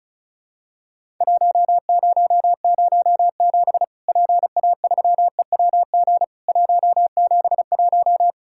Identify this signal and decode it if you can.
Morse code, heard as a keyed tone, about 36 wpm, 700 Hz. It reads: 1007 PA3EWG 171
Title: Worked All Europe DX Contest - QTC Exchange - 35wpm